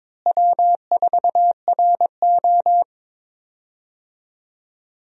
Morse code, heard as a keyed tone, about 23 wpm, 700 Hz. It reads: W4RO
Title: USA Call Signs - 4 Characters - Repeated 2x - 22wpm